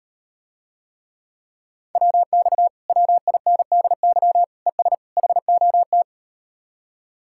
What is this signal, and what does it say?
Morse code, heard as a keyed tone, about 38 wpm, 700 Hz. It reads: WX WINDY ES HOT